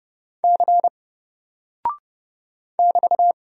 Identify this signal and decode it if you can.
Morse code, heard as a keyed tone, about 30 wpm, 700 Hz. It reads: C <BT>
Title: Single Character (Letter:Number:Punctuation:Prosign) Alt 30wpm